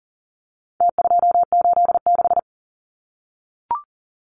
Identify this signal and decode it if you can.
Morse code, heard as a keyed tone, about 40 wpm, 700 Hz. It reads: T286
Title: Numbers - Contest Style - 4-Digit 40wpm